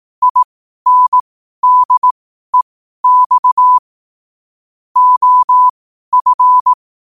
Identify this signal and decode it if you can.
Morse code, heard as a keyed tone, about 18 wpm, spaced as for 12 wpm, 1000 Hz. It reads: INDEX OF
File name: qotdfiles / 12wpm